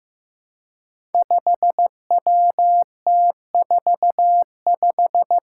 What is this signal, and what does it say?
Morse code, heard as a keyed tone, about 15 wpm, 700 Hz. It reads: 5WT45
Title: Five Challenging Letter-Number Combinations 15wpm